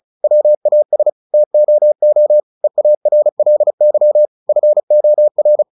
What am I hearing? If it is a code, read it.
Morse code, heard as a keyed tone, about 34 wpm, 600 Hz. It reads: WAS TOO EARLY FOR
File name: the old man and the sea 2 35wpm